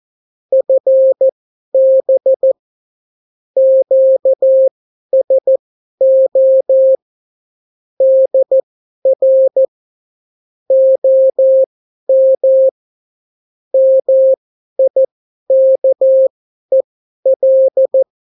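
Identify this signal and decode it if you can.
Morse code, heard as a keyed tone, about 14 wpm, spaced as for 11 wpm, 530 Hz. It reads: FB QSO DR OM MIKEL
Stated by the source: Example Spanish CW QSO from a Morse learning book, 14 wpm characters with 8 wpm spacing.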